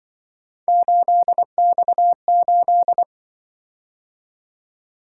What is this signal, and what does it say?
Morse code, heard as a keyed tone, about 24 wpm, 700 Hz. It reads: 8X8
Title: Speed-Racing - Three Letter-Number Combinations - 20wpm